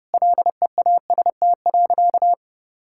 Morse code, heard as a keyed tone, about 29 wpm, 700 Hz. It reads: LEAST.
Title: Sentences from Top 1000 Words - Review Part 5 - 30wpm